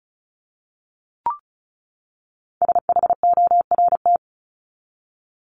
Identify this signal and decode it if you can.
Morse code, heard as a keyed tone, about 34 wpm, 700 Hz. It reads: SHORT
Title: LY Suffix - Top 100 Words with Base Words 35wpm